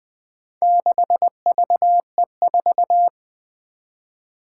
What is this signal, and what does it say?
Morse code, heard as a keyed tone, about 20 wpm, 700 Hz. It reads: 6VE4
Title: Four Challenging Letter-Number Combinations 20wpm